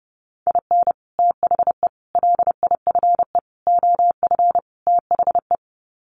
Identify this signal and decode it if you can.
Morse code, heard as a keyed tone, about 28 wpm, 700 Hz. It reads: IN THE LIFE OF THE
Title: Sets of 5 Words Encore2 30wpm